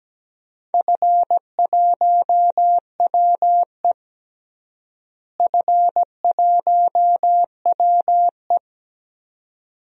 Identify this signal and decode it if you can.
Morse code, heard as a keyed tone, about 17 wpm, 700 Hz. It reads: F1WE F1WE
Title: DXCC-100 Call Signs - 4 Characters - Repeated 2x - 17wpm